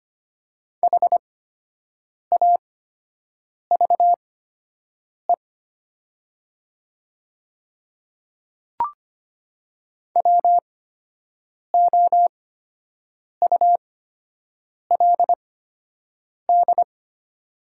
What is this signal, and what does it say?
Morse code, heard as a keyed tone, about 25 wpm, spaced as for 7 wpm, 700 Hz. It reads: HAVE WOULD
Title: 8x Character Spacing - Top 100 Words - 25wpm